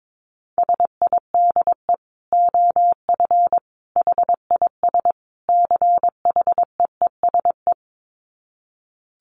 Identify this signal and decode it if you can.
Morse code, heard as a keyed tone, about 22 wpm, 700 Hz. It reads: SIDE OF HIS CHEESE
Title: Dit Challenge 22wpm